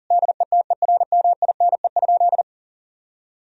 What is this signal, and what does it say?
Morse code, heard as a keyed tone, about 40 wpm, 700 Hz. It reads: DETERMINE?
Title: Sentences from Top 1000 Words - Review Part 4 - 40wpm